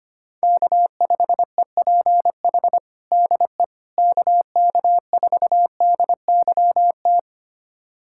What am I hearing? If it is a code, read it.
Morse code, heard as a keyed tone, about 25 wpm, 700 Hz. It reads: K5EPH DE KK4DYT
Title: USA Callsign DE Callsign 25wpm